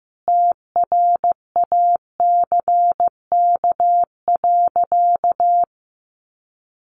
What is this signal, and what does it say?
Morse code, heard as a keyed tone, about 15 wpm, 700 Hz. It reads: TRACK.
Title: Sentences from Top 1100 Words 15wpm